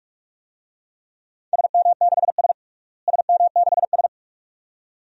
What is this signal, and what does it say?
Morse code, heard as a keyed tone, about 41 wpm, 700 Hz. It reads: SM6S SM6S
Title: DXCC-100 Call Signs - 4 Characters - Repeated 2x - 45wpm